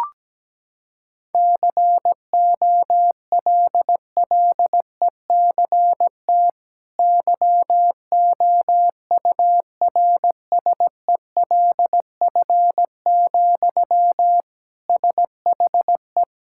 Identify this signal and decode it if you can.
Morse code, heard as a keyed tone, about 17 wpm, 700 Hz. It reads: COLLECT YOURSELF, SHE
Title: Sentences from Top 1600 Words 17wpm